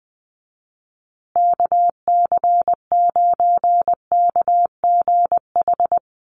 Morse code, heard as a keyed tone, about 20 wpm, 700 Hz. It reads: KC9KGH
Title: USA Call Signs Repeated 3x 20wpm